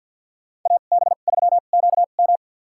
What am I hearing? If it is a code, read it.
Morse code, heard as a keyed tone, about 52 wpm, spaced as for 37 wpm, 700 Hz. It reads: AB3QM
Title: USA Call Signs - 5 Characters - 2x Character Spacing - 50wpm